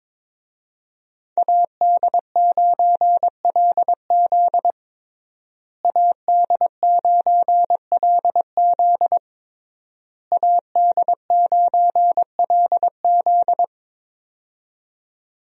Morse code, heard as a keyed tone, about 22 wpm, 700 Hz. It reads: AD9LZ AD9LZ AD9LZ
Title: USA Call Signs - 5 Characters - Repeated 3x - 22wpm